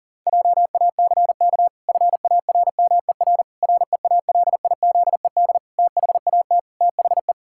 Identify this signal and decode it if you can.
Morse code, heard as a keyed tone, about 40 wpm, 700 Hz. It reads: JACK FARMER REALIZED THAT THE
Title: Pangram Challenge v2 - 40wpm